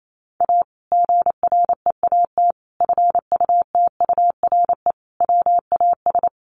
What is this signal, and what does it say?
Morse code, heard as a keyed tone, about 28 wpm, 700 Hz. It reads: A GREAT FUTURE WAS
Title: Sentences from Top 700 Words 28wpm